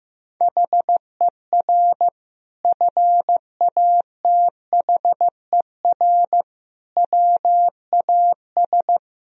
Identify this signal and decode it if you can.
Morse code, heard as a keyed tone, about 15 wpm, 700 Hz. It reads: HER FATHER WAS
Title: Sentences from Top 1000 Words - Review Part 4 - 15wpm